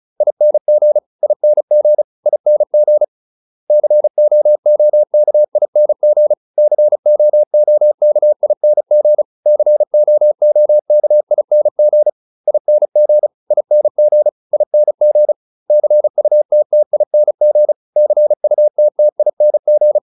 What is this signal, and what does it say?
Morse code, heard as a keyed tone, about 34 wpm, 600 Hz. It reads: ING ING ING COOKING COOKING COOKING ING ING ING CUTTING CUTTING